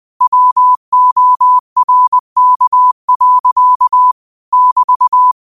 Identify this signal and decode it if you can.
Morse code, heard as a keyed tone, about 20 wpm, 1000 Hz. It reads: WORK. <BT>
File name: qotdfiles / 20wpm